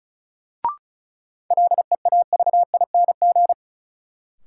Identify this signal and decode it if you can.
Morse code, heard as a keyed tone, about 36 wpm, 700 Hz. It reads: LEAVING
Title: Sentences from Top 1000 Words - Review Part 4 - New Word Drill - 35wpm